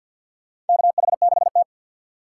Morse code, heard as a keyed tone, about 51 wpm, 700 Hz. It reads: KH6T
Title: USA Call Signs - 4 Characters - 2x Character Spacing - 50wpm